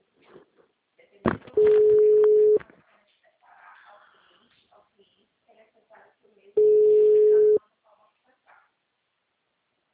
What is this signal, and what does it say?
Ringback tone (Europe, 425 Hz), with speech.